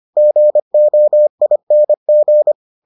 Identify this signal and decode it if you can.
Morse code, heard as a keyed tone, about 25 wpm, 600 Hz. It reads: GOING